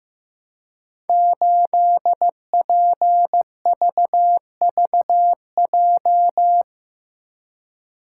Morse code, heard as a keyed tone, about 15 wpm, 700 Hz. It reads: 8PVVJ
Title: Code Groups - 5 Letter:Numbers - 15wpm